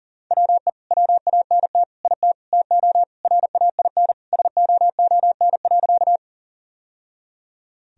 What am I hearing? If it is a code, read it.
Morse code, heard as a keyed tone, about 40 wpm, 700 Hz. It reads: WE WANT IT TO RAIN SOON.
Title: Sentences from Top 1000 Words - Review Part 1 - 40wpm